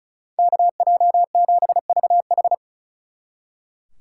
Morse code, heard as a keyed tone, about 46 wpm, 700 Hz. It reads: KJ7VH